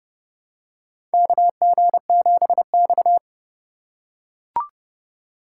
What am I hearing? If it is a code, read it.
Morse code, heard as a keyed tone, about 30 wpm, 700 Hz. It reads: KG7X
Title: USA Call Signs - 4 Characters - 4x Character Spacing - 30wpm